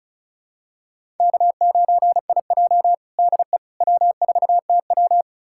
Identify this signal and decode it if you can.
Morse code, heard as a keyed tone, about 36 wpm, 700 Hz. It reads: K9IJ DE W4TW
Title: USA Callsign DE Callsign - 4 Characters - 35wpm